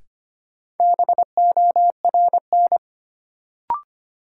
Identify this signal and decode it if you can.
Morse code, heard as a keyed tone, about 25 wpm, 700 Hz. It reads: BORN
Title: Sentences from Top 1000 Words - New Word Drill - 25wpm